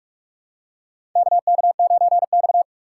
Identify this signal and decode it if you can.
Morse code, heard as a keyed tone, about 46 wpm, 700 Hz. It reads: KK9X
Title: USA Call Signs - 2x Character Spacing - 45wpm